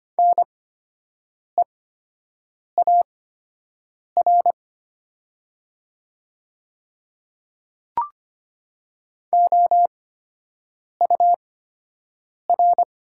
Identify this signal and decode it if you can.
Morse code, heard as a keyed tone, about 25 wpm, spaced as for 7 wpm, 690 Hz. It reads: NEAR OUR
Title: 8x Character Spacing - Top 500 Words - 25wpm